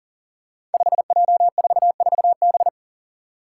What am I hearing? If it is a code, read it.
Morse code, heard as a keyed tone, about 40 wpm, 700 Hz. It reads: 5J44B
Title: Five Challenging Letter-Number Combinations 40wpm